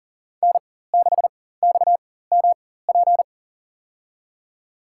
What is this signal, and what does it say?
Morse code, heard as a keyed tone, about 40 wpm, spaced as for 19 wpm, 700 Hz. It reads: N6XMP
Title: USA Call Signs - 4x Character Spacing - 40wpm